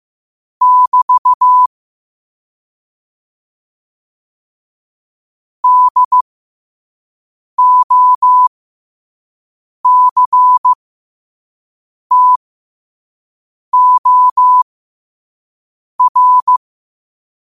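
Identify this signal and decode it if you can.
Morse code, heard as a keyed tone, about 15 wpm, spaced as for 5 wpm, 1000 Hz. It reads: <BT> DOCTOR
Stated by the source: Morse quote-of-the-day practice at 5 wpm.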